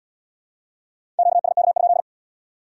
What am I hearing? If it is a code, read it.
Morse code, heard as a keyed tone, about 67 wpm, 700 Hz. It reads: QSK?